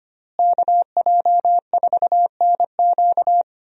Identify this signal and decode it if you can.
Morse code, heard as a keyed tone, about 25 wpm, 700 Hz. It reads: KJ4NQ
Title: USA Call Signs - 5 Characters - Repeated 3x - 25wpm